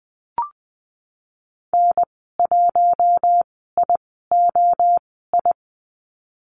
Morse code, heard as a keyed tone, about 20 wpm, spaced as for 15 wpm, 700 Hz. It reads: N1IOI
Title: 042 Lesson 8g - Number 1 - Call Signs 20wpm